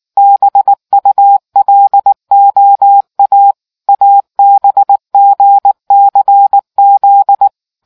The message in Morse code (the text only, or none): BULOA ABGCZ